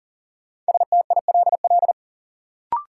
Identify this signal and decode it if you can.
Morse code, heard as a keyed tone, about 40 wpm, 700 Hz. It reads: STILL
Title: Sentences from Top 300 Words - New Word Drill - 40wpm